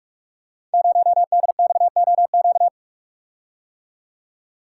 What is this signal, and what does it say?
Morse code, heard as a keyed tone, about 45 wpm, 700 Hz. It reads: 0DXOQ
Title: Code Groups - 5 Letter:Numbers - 45wpm